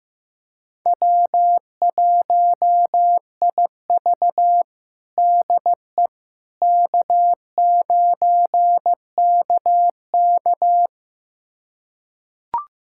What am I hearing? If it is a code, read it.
Morse code, heard as a keyed tone, about 16 wpm, 700 Hz. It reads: W1IV DE K9KK
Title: USA Callsign DE Callsign - 4 Characters - 15wpm